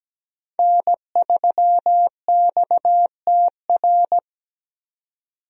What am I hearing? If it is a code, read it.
Morse code, heard as a keyed tone, about 17 wpm, 700 Hz. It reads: N3XTR
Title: USA Call Signs - 5 Characters - 17wpm